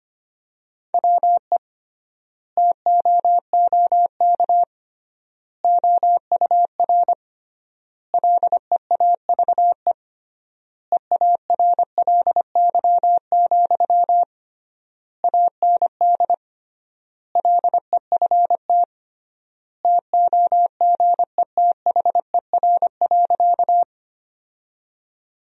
Morse code, heard as a keyed tone, about 25 wpm, 700 Hz. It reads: WE TOOK OUR LEAVE EARLY, AND LEFT TOGETHER.
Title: 3x Word-Spacing - Sentences from Top 400 Words - 25wpm